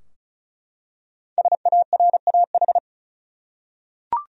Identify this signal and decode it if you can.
Morse code, heard as a keyed tone, about 34 wpm, 700 Hz. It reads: SARAH
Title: USA Common Names - Max 5 Letters - 35wpm